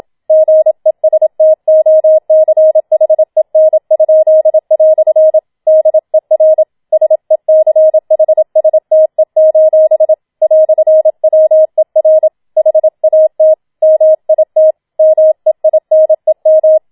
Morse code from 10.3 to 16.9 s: "WER HAT MIT MEINEM